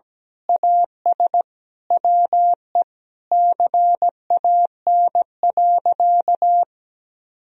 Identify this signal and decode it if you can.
Morse code, heard as a keyed tone, about 17 wpm, 700 Hz. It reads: AS WE CAN.
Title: Sentences from Top 1000 Words - Review Part 2 - 17wpm